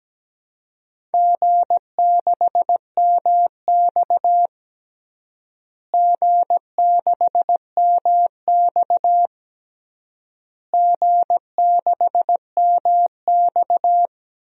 Morse code, heard as a keyed tone, about 17 wpm, 700 Hz. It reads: G6MX G6MX G6MX
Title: DXCC-100 Call Signs - 4 Characters - Repeated 3x - 17wpm